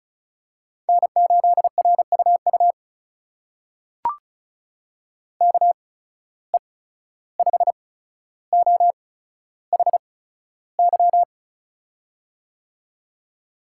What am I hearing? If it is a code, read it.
Morse code, heard as a keyed tone, about 36 wpm, spaced as for 14 wpm, 700 Hz. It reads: N8RUU KE5OHY
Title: USA Call Signs - 8x Character Spacing - 35wpm